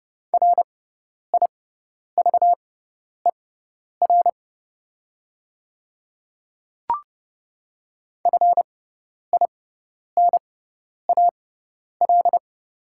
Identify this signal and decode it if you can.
Morse code, heard as a keyed tone, about 29 wpm, spaced as for 10 wpm, 690 Hz. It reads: RIVER FINAL
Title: 6x Character Spacing - Top 500 Words - 30wpm